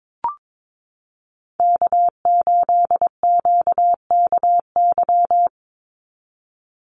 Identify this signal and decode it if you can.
Morse code, heard as a keyed tone, about 22 wpm, 700 Hz. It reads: K8QKY
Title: Call Signs Repeated 2x - 22wpm